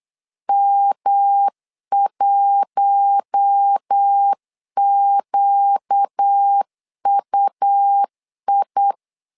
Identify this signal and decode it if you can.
Morse code, heard as a keyed tone, about 8 wpm, 800 Hz. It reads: M1QUI
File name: LEZIONI / 4 Q1S 8wpm 2710